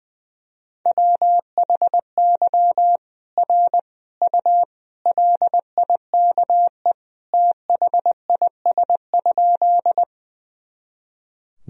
Morse code, heard as a keyed tone, about 20 wpm, 700 Hz. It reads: WHY R U LIKE THIS?
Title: Sentences from Top 200 Words and Abbreviations 20wpm